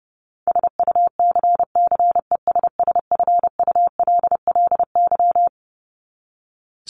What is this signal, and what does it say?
Morse code, heard as a keyed tone, about 30 wpm, 700 Hz. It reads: SUCCESSFULLY
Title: LY Suffix - Top 100 Words with Base Words 30wpm